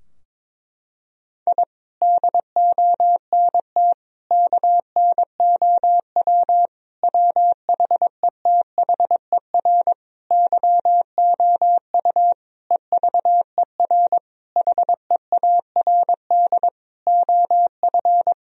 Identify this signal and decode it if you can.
Morse code, heard as a keyed tone, about 21 wpm, 700 Hz. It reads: I DONT KNOW WHETHER YOU EVER HEARD OF
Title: Sentences from Top 1000 Words - Review Part 3 - 22wpm